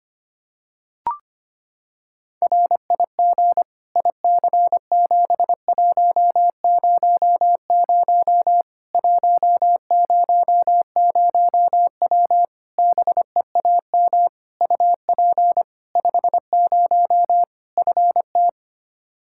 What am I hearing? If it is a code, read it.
Morse code, heard as a keyed tone, about 25 wpm, 700 Hz. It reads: RIG IC7100 100W BEAM UP 50 FT
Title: Practice QSO - Exchange 2 - Info - 25wpm